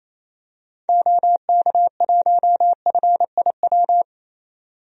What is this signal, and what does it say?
Morse code, heard as a keyed tone, about 28 wpm, 700 Hz. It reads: OK1FIW
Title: DXCC-100 Call Signs - 6 Characters - 28wpm